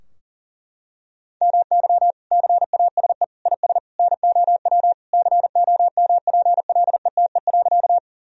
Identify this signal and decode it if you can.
Morse code, heard as a keyed tone, about 37 wpm, 700 Hz. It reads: MY CASE IS NOW COMPLETE.
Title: Sentences from Top 1000 Words - Review Part 2 - 40wpm